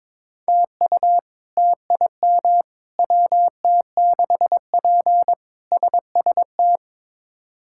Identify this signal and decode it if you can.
Morse code, heard as a keyed tone, about 22 wpm, 700 Hz. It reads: TU TIM WT6P SST
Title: K1USN SST QSO Practice - 22wpm